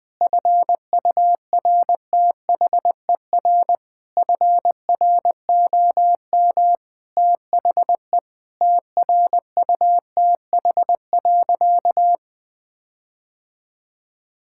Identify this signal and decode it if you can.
Morse code, heard as a keyed tone, about 20 wpm, 700 Hz. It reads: FURTHER FROM THE TRUTH.
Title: Sentences with 7-Word Phrases - Spoken Beginning - 20wpm